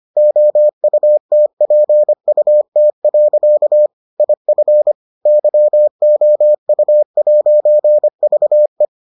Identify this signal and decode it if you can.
Morse code, heard as a keyed tone, about 25 wpm, 600 Hz. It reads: OUTPUT. IF YOU'VE